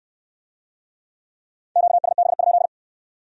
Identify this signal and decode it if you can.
Morse code, heard as a keyed tone, about 65 wpm, 700 Hz. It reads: QSB?